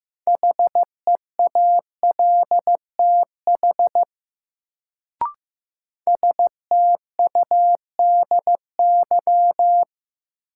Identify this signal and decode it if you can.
Morse code, heard as a keyed tone, about 15 wpm, 700 Hz. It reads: HEALTH STUDY
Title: Sets of 5 Words - New Word Drill - 15wpm